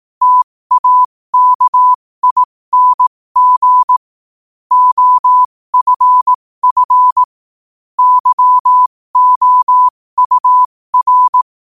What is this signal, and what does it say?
Morse code, heard as a keyed tone, about 18 wpm, spaced as for 15 wpm, 1000 Hz. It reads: TAKING OFF YOUR
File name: qotdfiles / 15wpm